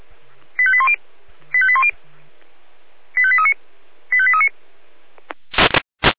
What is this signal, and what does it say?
Five-tone selective call (ZVEI), heard as quick runs of five tones, tones of about 70 ms. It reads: 85810 85810 85839 85839